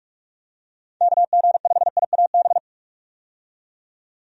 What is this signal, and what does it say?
Morse code, heard as a keyed tone, about 46 wpm, 700 Hz. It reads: KG5IAB
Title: USA Call Signs - 6 Characters - 45wpm